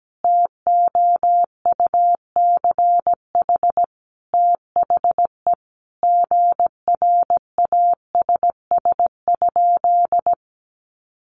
Morse code, heard as a keyed tone, about 17 wpm, 700 Hz. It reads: TOUCH THE GRASS?
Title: A Sound Of Thunder 17wpm